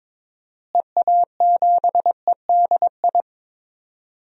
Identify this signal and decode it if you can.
Morse code, heard as a keyed tone, about 22 wpm, 700 Hz. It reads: EA7EDI
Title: DXCC-100 Call Signs - 6 Characters - Repeated 2x - 22wpm